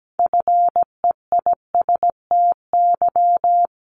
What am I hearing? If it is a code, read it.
Morse code, heard as a keyed tone, about 17 wpm, 700 Hz. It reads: FEISTY
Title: Multi-letter Phonemes 17wpm